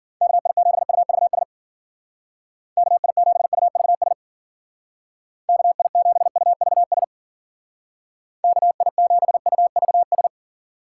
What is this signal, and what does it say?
Morse code, heard as a keyed tone, about 50 wpm, 700 Hz. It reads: KI7UVS KI7UVS KI7UVS KI7UVS